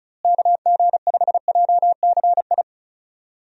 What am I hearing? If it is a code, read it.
Morse code, heard as a keyed tone, about 34 wpm, 700 Hz. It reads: KG5JCI
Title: USA Call Signs - 6 Characters - Repeated 3x - 35wpm